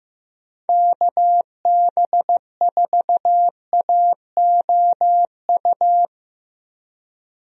Morse code, heard as a keyed tone, about 15 wpm, 700 Hz. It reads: KB4AOU
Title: USA Call Signs - 6 Characters - Repeated 3x - 15wpm